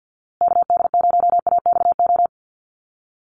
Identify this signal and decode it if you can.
Morse code, heard as a keyed tone, about 50 wpm, 700 Hz. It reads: KD0AXO
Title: USA Call Signs Repeated 2x 50wpm